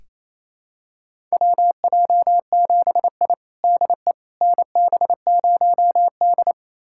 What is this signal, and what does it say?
Morse code, heard as a keyed tone, about 27 wpm, 700 Hz. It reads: WJ7I DE NB0D
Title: USA Callsign DE Callsign - 4 Characters - 28wpm